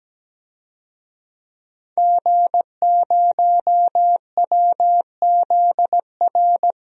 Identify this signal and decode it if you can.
Morse code, heard as a keyed tone, about 17 wpm, 700 Hz. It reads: G0WZR
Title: DXCC-100 Call Signs - 5 Characters - 17wpm